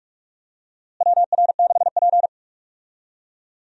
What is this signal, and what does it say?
Morse code, heard as a keyed tone, about 46 wpm, 700 Hz. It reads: WR6P
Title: USA Call Signs - 4 Characters - Repeated 2x - 45wpm